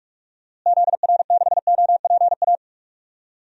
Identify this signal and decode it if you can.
Morse code, heard as a keyed tone, about 44 wpm, 700 Hz. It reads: ZR6OPA
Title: DXCC-100 Call Signs - 6 Characters - Repeated 2x - 45wpm